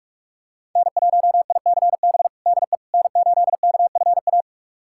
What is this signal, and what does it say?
Morse code, heard as a keyed tone, about 46 wpm, 700 Hz. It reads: N1ICB DE N8KFA
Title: USA Callsign DE Callsign - 5 Characters - 45wpm